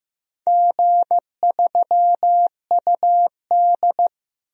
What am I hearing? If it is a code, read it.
Morse code, heard as a keyed tone, about 15 wpm, 700 Hz. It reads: G3UD
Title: DXCC-100 Call Signs - 4 Characters - 15wpm